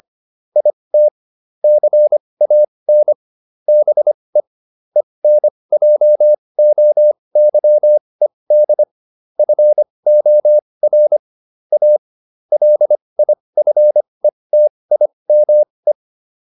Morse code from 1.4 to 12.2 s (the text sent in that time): CAN BE ENJOYED FOR A